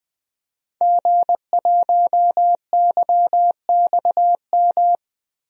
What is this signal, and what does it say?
Morse code, heard as a keyed tone, about 20 wpm, 700 Hz. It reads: G1YXM
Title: DXCC-100 Call Signs - 5 Characters - 20wpm